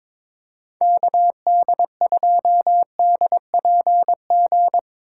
Morse code, heard as a keyed tone, about 22 wpm, 700 Hz. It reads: KD2DPG
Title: USA Call Signs - 6x Character Spacing - 22wpm